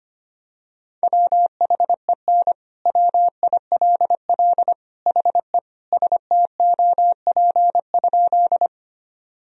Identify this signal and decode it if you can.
Morse code, heard as a keyed tone, about 25 wpm, 700 Hz. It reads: WHEN WILL HE STOP?
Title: Sentences from Top 1000 Words - Review Part 1 - 25wpm